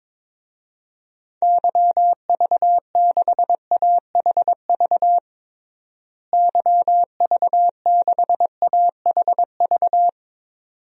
Morse code, heard as a keyed tone, about 22 wpm, 700 Hz. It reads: YV6AHV YV6AHV
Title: DXCC-100 Call Signs - 6 Characters - Repeated 2x - 22wpm